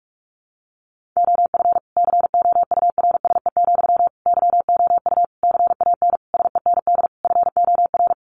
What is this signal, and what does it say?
Morse code, heard as a keyed tone, about 44 wpm, 700 Hz. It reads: OF COURSE, YOU CAN SEND FOR
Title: Sentences from Top 1500 Words 45wpm